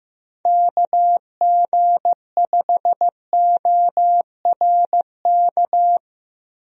Morse code, heard as a keyed tone, about 15 wpm, 700 Hz. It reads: KG5ORK
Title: USA Call Signs - 6 Characters - Repeated 2x - 15wpm